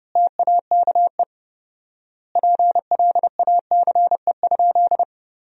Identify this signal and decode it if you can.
Morse code, heard as a keyed tone, about 32 wpm, 700 Hz. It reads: TAKE PLACE?
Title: 4x Word-Spacing - Sentences from Top 200 Words - 30wpm